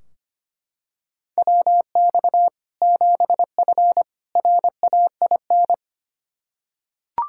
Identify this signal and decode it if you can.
Morse code, heard as a keyed tone, about 25 wpm, 700 Hz. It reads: WX 7F RAIN
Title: K1USN SST QSO Practice 25wpm